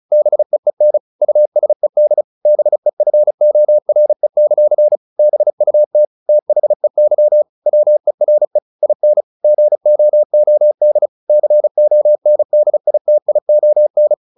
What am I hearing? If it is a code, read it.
Morse code, heard as a keyed tone, about 34 wpm, 600 Hz. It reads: BEEN USED BEFORE; BUT THEY WERE IN GOOD CONDITION
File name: the old man and the sea / the old man and the sea 1 35wpm